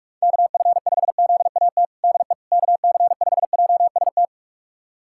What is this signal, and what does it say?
Morse code, heard as a keyed tone, about 41 wpm, 700 Hz. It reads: KF5ZAT DE KC5JST
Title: USA Callsign DE Callsign - 6 Characters - 45wpm